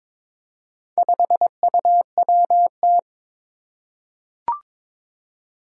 Morse code, heard as a keyed tone, about 22 wpm, 700 Hz. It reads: 5UWT
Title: Four Challenging Letter-Number Combinations 22wpm